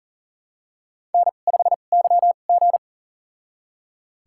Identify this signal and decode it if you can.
Morse code, heard as a keyed tone, about 40 wpm, spaced as for 29 wpm, 700 Hz. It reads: N5YG
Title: USA Call Signs - 4 Characters - 2x Character Spacing - 40wpm